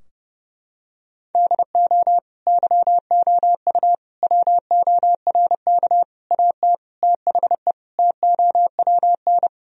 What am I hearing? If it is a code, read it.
Morse code, heard as a keyed tone, about 30 wpm, 700 Hz. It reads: DO YOU WORK AT THE TOWN
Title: Sentences from Top 1000 Words - Review Part 2 - 30wpm